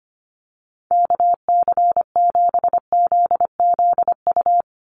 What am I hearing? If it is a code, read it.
Morse code, heard as a keyed tone, about 25 wpm, 700 Hz. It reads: KC7ZZU
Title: USA Call Signs 25wpm